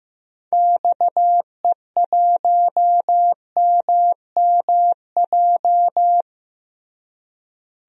Morse code, heard as a keyed tone, about 15 wpm, 700 Hz. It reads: XE1MMJ
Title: DXCC-100 Call Signs - 6 Characters - Repeated 2x - 15wpm